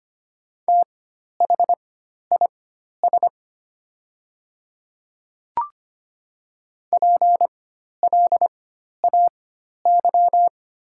Morse code, heard as a keyed tone, about 25 wpm, spaced as for 12 wpm, 700 Hz. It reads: THIS PLAY
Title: 4x Character Spacing - Top 500 Words - 25wpm